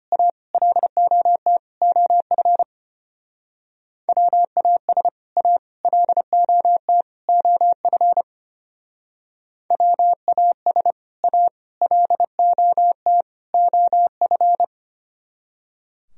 Morse code, heard as a keyed tone, about 33 wpm, 700 Hz. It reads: A LOT OF WAS A LOT OF WAS A LOT OF